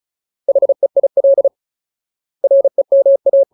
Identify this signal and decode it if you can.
Morse code, heard as a keyed tone, about 34 wpm, 550 Hz. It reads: HEIL REMA
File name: Ultimate Word List / 35WPM Ultimate